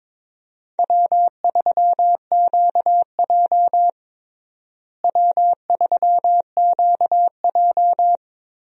Morse code, heard as a keyed tone, about 22 wpm, 700 Hz. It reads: W3QJ W3QJ
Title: USA Call Signs - 4 Characters - Repeated 3x - 22wpm